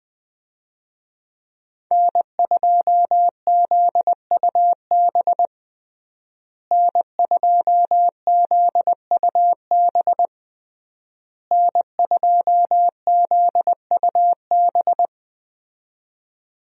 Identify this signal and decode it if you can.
Morse code, heard as a keyed tone, about 21 wpm, 700 Hz. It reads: N2ZUB N2ZUB N2ZUB
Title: USA Call Signs - 5 Characters - Repeated 3x - 20wpm